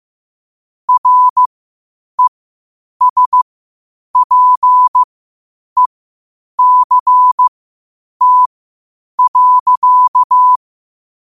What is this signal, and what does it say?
Morse code, heard as a keyed tone, about 15 wpm, spaced as for 8 wpm, 1000 Hz. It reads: RESPECT.
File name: qotdfiles / 8wpm